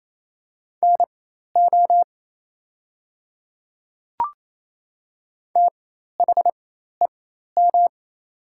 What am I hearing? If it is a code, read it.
Morse code, heard as a keyed tone, about 28 wpm, spaced as for 13 wpm, 690 Hz. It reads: NO THEM
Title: 4x Character Spacing - Top 100 Words - 28wpm